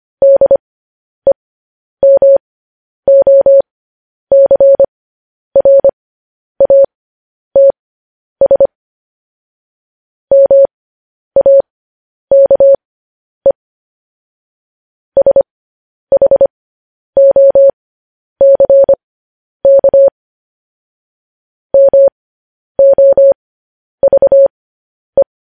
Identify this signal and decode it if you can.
Morse code, heard as a keyed tone, about 25 wpm, spaced as for 10 wpm, 550 Hz. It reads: DEMOCRATS MAKE SHOCK MOVE
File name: podcast / News headlines-1x-25x10